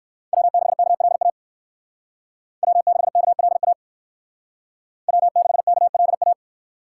Morse code, heard as a keyed tone, about 58 wpm, 700 Hz. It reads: W6KLA W6KLA W6KLA